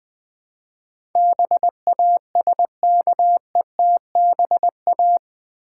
Morse code, heard as a keyed tone, about 20 wpm, 700 Hz. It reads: BASKETBA
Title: Build A Word - 4 Letters At A Time - 10-Letter Words - 20wpm